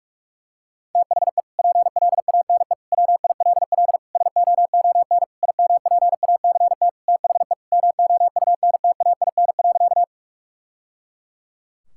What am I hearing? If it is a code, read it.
Morse code, heard as a keyed tone, about 46 wpm, 700 Hz. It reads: THE PLANE WILL SOON IMPACT THE MOUNTAIN.
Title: Sentences from Top 1000 Words - Review Part 6 - 45wpm